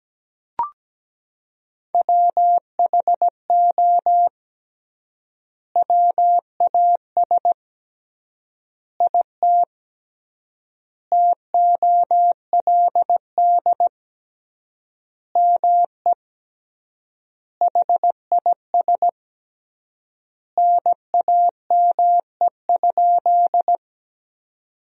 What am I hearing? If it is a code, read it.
Morse code, heard as a keyed tone, about 17 wpm, 700 Hz. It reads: WHO WAS IT TOLD ME HIS NAME?
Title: 3x Word-Spacing - Sentences from Top 400 Words - 17wpm